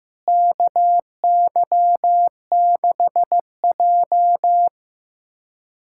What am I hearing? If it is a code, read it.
Morse code, heard as a keyed tone, about 15 wpm, 700 Hz. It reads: KY6J
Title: USA Callsign DE Callsign - 4 Characters - 15wpm